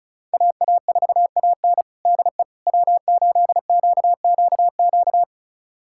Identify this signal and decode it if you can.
Morse code, heard as a keyed tone, about 36 wpm, 700 Hz. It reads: AA4AN DE W8QQQ
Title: USA Callsign DE Callsign 35wpm